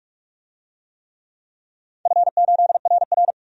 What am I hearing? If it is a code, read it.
Morse code, heard as a keyed tone, about 46 wpm, 700 Hz. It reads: F8RR
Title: DXCC-100 Call Signs - 4 Characters - 45wpm